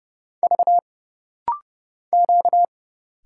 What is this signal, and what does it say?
Morse code, heard as a keyed tone, about 30 wpm, 700 Hz. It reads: V Q
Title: Single Letters - Rapid-Fire 30wpm